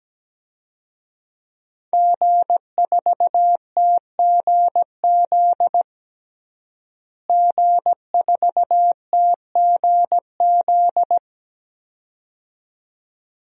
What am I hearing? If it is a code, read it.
Morse code, heard as a keyed tone, about 17 wpm, 700 Hz. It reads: G4TGZ G4TGZ
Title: DXCC-100 Call Signs - 5 Characters - Repeated 2x - 17wpm